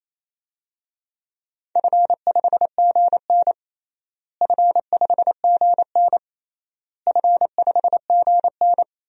Morse code, heard as a keyed tone, about 28 wpm, 700 Hz. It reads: F5GN F5GN F5GN
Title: DXCC-100 Call Signs - 4 Characters - Repeated 3x - 28wpm